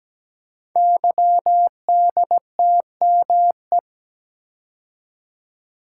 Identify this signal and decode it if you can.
Morse code, heard as a keyed tone, about 17 wpm, 700 Hz. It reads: YDTME
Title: Code Groups - 5 Letters - 17wpm